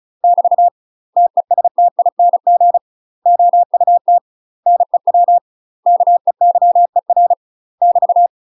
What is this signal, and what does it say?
Morse code, heard as a keyed tone, about 36 wpm, 700 Hz. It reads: <BT> TESTING OUT NEW KEYER <BT>
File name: UK MASTER QSOs TRAINING / 35WPM / UK MASTER QSOs 0041